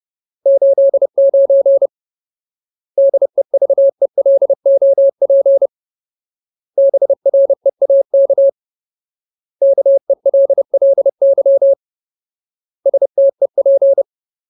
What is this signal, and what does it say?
Morse code, heard as a keyed tone, about 30 wpm, 550 Hz. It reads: 89 DEVELOP BREAK KELLY STEP
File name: Ultimate Word List / 30WPM Ultimate